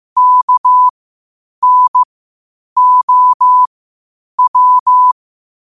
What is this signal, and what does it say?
Morse code, heard as a keyed tone, about 15 wpm, spaced as for 8 wpm, 1000 Hz. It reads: KNOW
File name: qotdfiles / 8wpm